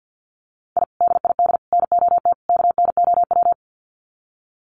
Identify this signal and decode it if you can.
Morse code, heard as a keyed tone, about 50 wpm, 700 Hz. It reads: I DID NOT KNOW
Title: Sets of 4 Words 50wpm